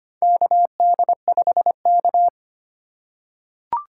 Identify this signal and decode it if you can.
Morse code, heard as a keyed tone, about 25 wpm, 700 Hz. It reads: KD5K
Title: USA Call Signs - 4 Characters - 6x Character Spacing - 25wpm